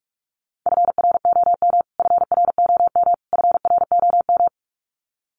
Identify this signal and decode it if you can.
Morse code, heard as a keyed tone, about 44 wpm, 700 Hz. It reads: FROM FROM FROM
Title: Top 500 Words Repeated 45wpm